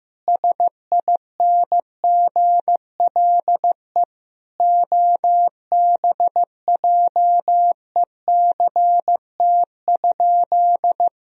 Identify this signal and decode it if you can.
Morse code, heard as a keyed tone, about 16 wpm, 700 Hz. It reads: SINGLE OBJECT?
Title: Sentences from Top 1000 Words - Review Part 4 - 15wpm